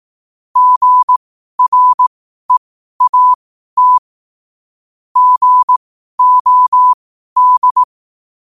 Morse code, heard as a keyed tone, about 18 wpm, spaced as for 12 wpm, 1000 Hz. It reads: GREAT GOD
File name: qotdfiles / 12wpm